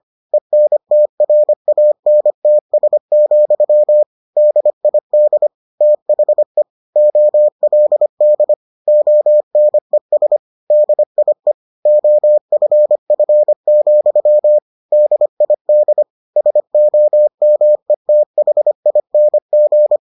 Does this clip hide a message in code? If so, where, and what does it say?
Yes: Morse code from 0.3 to 20.0 s, heard as a keyed tone, about 25 wpm, 600 Hz. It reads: ENTRANTS, DID THE OLD ONES DIE OFF, DID SOMETHING